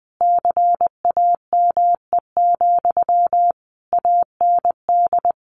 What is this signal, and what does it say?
Morse code, heard as a keyed tone, about 19 wpm, 700 Hz. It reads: CAME, AND
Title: Sentences from Top 1500 Words 20wpm